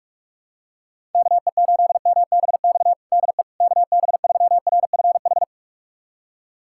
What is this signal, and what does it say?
Morse code, heard as a keyed tone, about 46 wpm, 700 Hz. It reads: KE8MBX DE KB3RFH
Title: USA Callsign DE Callsign 45wpm